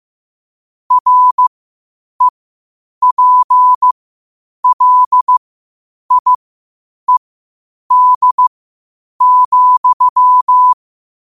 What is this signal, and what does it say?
Morse code, heard as a keyed tone, about 15 wpm, spaced as for 8 wpm, 1000 Hz. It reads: REPLIED,
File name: qotdfiles / 8wpm